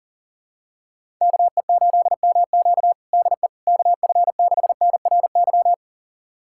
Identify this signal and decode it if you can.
Morse code, heard as a keyed tone, about 40 wpm, 700 Hz. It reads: KE8MQ DE KF6NRY
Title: USA Callsign DE Callsign 40wpm